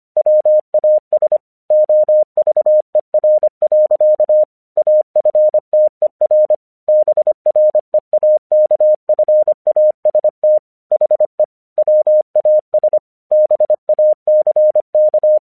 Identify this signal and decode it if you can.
Morse code, heard as a keyed tone, about 25 wpm, 610 Hz. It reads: WAS OVER. AFTER BREAKFAST HE WAS BACK
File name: English 25wpm